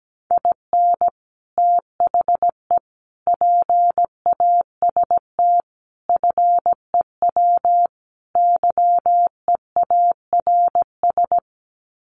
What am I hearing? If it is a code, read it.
Morse code, heard as a keyed tone, about 17 wpm, 700 Hz. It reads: IN THE PAST FEW YEARS
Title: Sets of 5 Words 17wpm